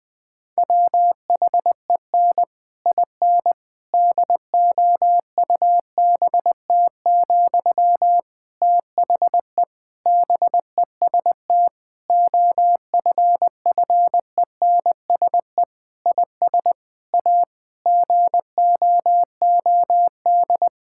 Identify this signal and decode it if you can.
Morse code, heard as a keyed tone, about 20 wpm, 700 Hz. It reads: WHEN IN DOUBT, THE BEST OFFENSE IS A GOOD
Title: Difficult Sentences with Hints - 20wpm